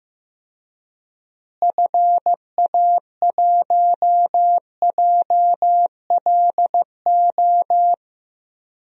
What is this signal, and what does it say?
Morse code, heard as a keyed tone, about 15 wpm, 700 Hz. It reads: FA1JLO
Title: DXCC-100 Call Signs - 6 Characters - Repeated 2x - 15wpm